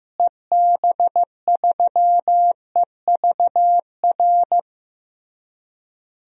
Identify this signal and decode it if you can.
Morse code, heard as a keyed tone, about 15 wpm, 700 Hz. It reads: EB3EVR
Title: DXCC-100 Call Signs - 6 Characters - Repeated 3x - 15wpm